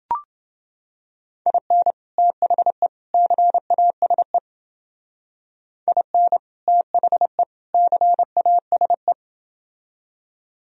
Morse code, heard as a keyed tone, about 29 wpm, 700 Hz. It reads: IN THE CASE IN THE CASE